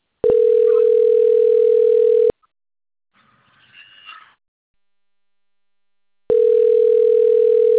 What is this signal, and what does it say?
Ringback tone (North America), 440 + 480 Hz.